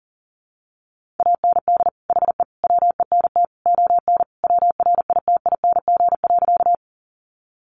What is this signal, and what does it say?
Morse code, heard as a keyed tone, about 40 wpm, 700 Hz. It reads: AND HE WENT ON WRITING.
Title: Sentences from Top 800 Words 40wpm